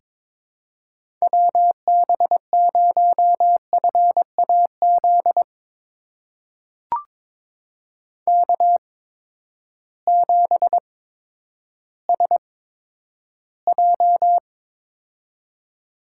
Morse code, heard as a keyed tone, about 22 wpm, spaced as for 11 wpm, 700 Hz. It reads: WB0FAZ K7SJ
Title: USA Call Signs - 8x Character Spacing - 22wpm